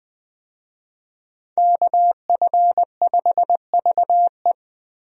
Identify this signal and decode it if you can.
Morse code, heard as a keyed tone, about 20 wpm, 700 Hz. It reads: KF5VE
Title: USA Call Signs - 5 Characters - Repeated 2x - 20wpm